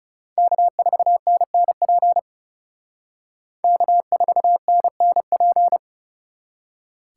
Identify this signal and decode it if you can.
Morse code, heard as a keyed tone, about 37 wpm, 700 Hz. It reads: K4NNP K4NNP
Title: Speed Racing - USA 1x3 Call Signs 30wpm